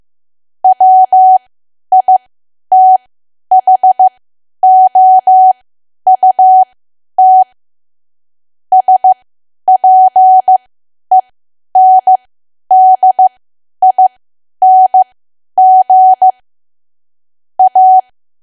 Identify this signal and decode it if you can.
Morse code, heard as a keyed tone, about 15 wpm, spaced as for 10 wpm, 750 Hz. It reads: WITHOUT SPENDING A
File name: cwa / W1AW-Level2 / 10WPM